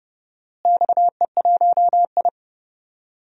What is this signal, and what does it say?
Morse code, heard as a keyed tone, about 28 wpm, 700 Hz. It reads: XE1I
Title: DXCC-100 Call Signs - 4 Characters - Repeated 3x - 30wpm